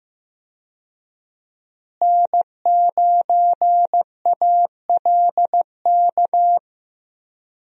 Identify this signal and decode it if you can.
Morse code, heard as a keyed tone, about 15 wpm, 700 Hz. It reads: N9ALK
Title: USA Call Signs - 5 Characters - Repeated 3x - 15wpm